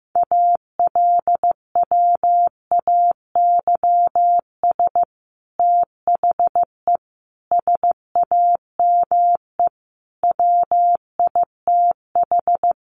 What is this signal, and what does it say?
Morse code, heard as a keyed tone, about 15 wpm, 700 Hz. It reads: ALWAYS THE SAME WITH
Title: Sentences from Top 300 Words 15wpm